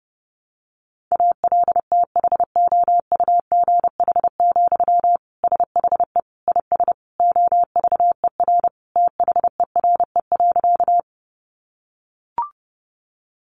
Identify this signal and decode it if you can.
Morse code, heard as a keyed tone, about 30 wpm, 700 Hz. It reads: ALTHOUGH, SHE IS OVER THERE.
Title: Sentences from Top 600 Words 30wpm